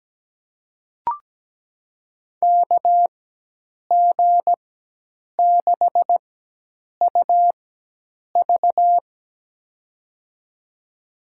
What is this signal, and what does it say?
Morse code, heard as a keyed tone, about 17 wpm, spaced as for 8 wpm, 700 Hz. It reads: KG6UV
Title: USA Call Signs - 5 Characters - 4x Character Spacing - 17wpm